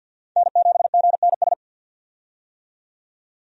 Morse code, heard as a keyed tone, about 53 wpm, 700 Hz. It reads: N7GNS
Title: USA Call Signs - 5 Characters - 50wpm